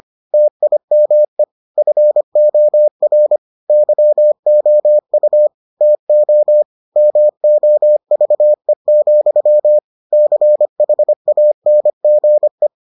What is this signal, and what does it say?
Morse code, heard as a keyed tone, about 25 wpm, 600 Hz. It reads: TIME FOR YOU TO MOVE, CHANGE